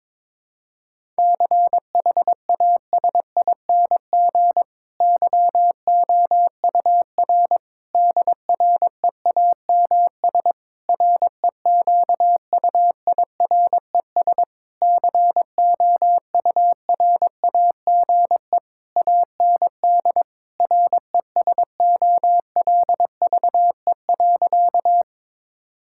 Morse code, heard as a keyed tone, about 22 wpm, 700 Hz. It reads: CHASING YOUR DREAMS REQUIRES COURAGE AND RESOLVE.
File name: Difficult Sentences with Hints - 22wpm